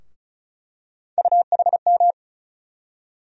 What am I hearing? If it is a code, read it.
Morse code, heard as a keyed tone, about 36 wpm, 700 Hz. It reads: UHM
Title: USA Call Sign Postfixes - 3 Letters 35wpm